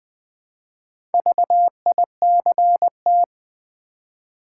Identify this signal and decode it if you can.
Morse code, heard as a keyed tone, about 20 wpm, 700 Hz. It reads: VICT
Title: Build A Word - 2 Letters At A Time - 6-Letter Words - 20wpm